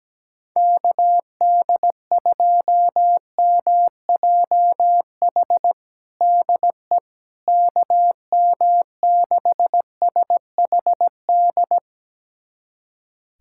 Morse code, heard as a keyed tone, about 17 wpm, 700 Hz. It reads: KD2MJH DE KM6SHD
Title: USA Callsign DE Callsign 17wpm